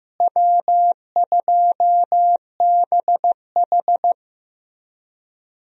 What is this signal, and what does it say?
Morse code, heard as a keyed tone, about 15 wpm, 700 Hz. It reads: W2BH
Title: USA Call Signs - 4 Characters - Repeated 3x - 15wpm